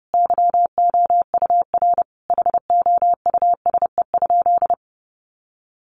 Morse code, heard as a keyed tone, about 30 wpm, 700 Hz. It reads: YOUR HOUSE?
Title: Sentences from Top 1500 Words 30wpm